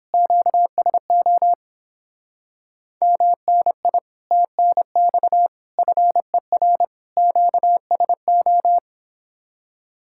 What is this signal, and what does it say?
Morse code, heard as a keyed tone, about 29 wpm, 700 Hz. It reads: QSO MNI TNX FER QSO